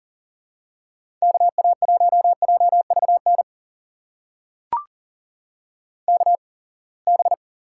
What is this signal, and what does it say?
Morse code, heard as a keyed tone, about 38 wpm, spaced as for 27 wpm, 700 Hz. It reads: KA1JVN KB
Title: USA Call Signs - 6 Characters - 8x Character Spacing - 40wpm